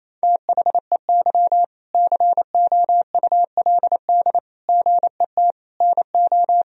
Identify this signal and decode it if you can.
Morse code, heard as a keyed tone, about 27 wpm, 700 Hz. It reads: THEY COULD GET NO
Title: Sentences from Top 1000 Words - Review Part 3 - 28wpm